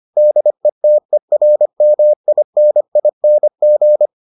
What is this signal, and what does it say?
Morse code, heard as a keyed tone, about 25 wpm, 600 Hz. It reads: DETERMINING